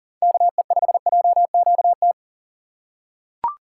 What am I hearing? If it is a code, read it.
Morse code, heard as a keyed tone, about 37 wpm, 700 Hz. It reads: KE5JQT
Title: USA Call Signs - 6 Characters - 2x Character Spacing - 40wpm